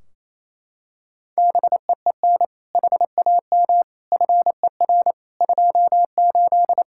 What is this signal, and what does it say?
Morse code, heard as a keyed tone, about 28 wpm, 700 Hz. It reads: BEEN HAM FER 28
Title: K1USN SST QSO Practice 28wpm